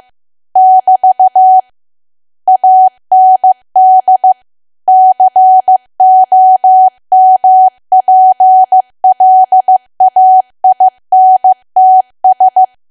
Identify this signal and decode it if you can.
Morse code, heard as a keyed tone, about 15 wpm, 750 Hz. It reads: <BT> AND COMPLAINTS
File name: cwa / W1AW-Level2 / 15WPM